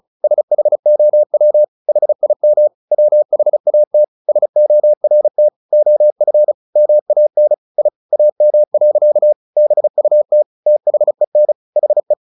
Morse code, heard as a keyed tone, about 34 wpm, 600 Hz. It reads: SHOW HIM WHAT SORT OF MAN I AM. BUT THEN HE
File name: the old man and the sea / the old man and the sea 1 35wpm